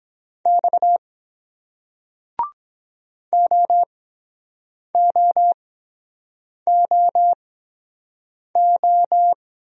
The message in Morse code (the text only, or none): X OOOO